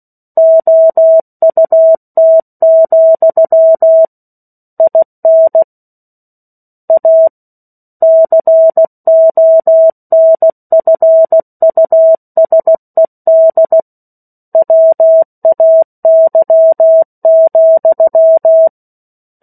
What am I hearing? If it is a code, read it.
Morse code, heard as a keyed tone, about 16 wpm, 650 Hz. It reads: OUT, IN A CONFUSED WAY,
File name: 03-Chapter-03 16 WPM